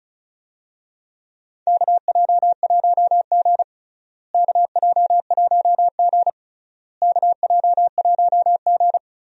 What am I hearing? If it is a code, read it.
Morse code, heard as a keyed tone, about 36 wpm, 700 Hz. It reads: KJ1G KJ1G KJ1G
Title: USA Call Signs - 4 Characters - Repeated 3x - 35wpm